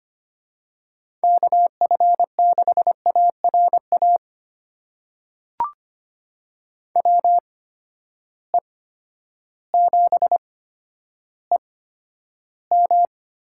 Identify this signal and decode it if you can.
Morse code, heard as a keyed tone, about 25 wpm, spaced as for 11 wpm, 700 Hz. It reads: KF6ARA WE7EM
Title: USA Call Signs - 8x Character Spacing - 25wpm